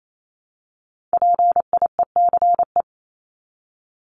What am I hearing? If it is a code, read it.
Morse code, heard as a keyed tone, about 28 wpm, 700 Hz. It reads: PIECE
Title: Top 500 Words 28wpm